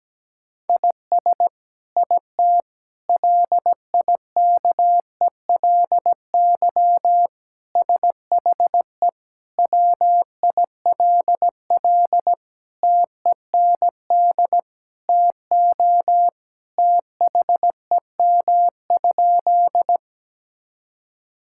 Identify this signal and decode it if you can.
Morse code, heard as a keyed tone, about 17 wpm, 700 Hz. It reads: IS IT LIKELY SHE WILL TEND TO THEM?
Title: Sentences from Top 1000 Words - Review Part 5 - 17wpm